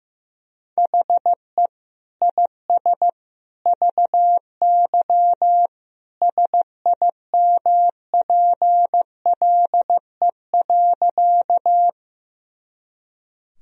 Morse code, heard as a keyed tone, about 15 wpm, 700 Hz. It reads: HE IS VY SIMPLE.
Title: Sentences from Top 400 Words and Abbreviations 15wpm